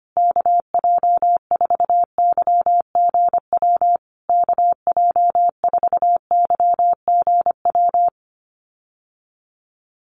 Morse code, heard as a keyed tone, about 25 wpm, 700 Hz. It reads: KJ4YGW KJ4YGW
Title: USA Call Signs Repeated 3x 25wpm